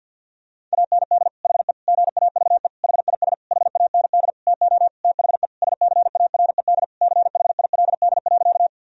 Morse code, heard as a keyed tone, about 50 wpm, 700 Hz. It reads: AND HE GAVE HIS HAND TO THE SCARED CHILD.
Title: Sentences from Top 1000 Words - Review Part 4 - 50wpm